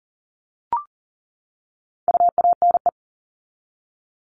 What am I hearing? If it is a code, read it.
Morse code, heard as a keyed tone, about 40 wpm, 700 Hz. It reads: UANE
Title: All Cut Numbers - 4-Digit 40wpm